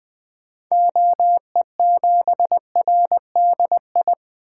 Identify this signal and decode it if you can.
Morse code, heard as a keyed tone, about 20 wpm, 700 Hz. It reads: OE7RDI
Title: 2023 SOTA Activator Call Signs - 20wpm